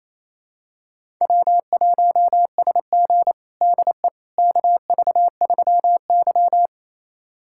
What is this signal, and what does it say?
Morse code, heard as a keyed tone, about 28 wpm, 700 Hz. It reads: W1SG DE KV3Y
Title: USA Callsign DE Callsign - 4 Characters - 28wpm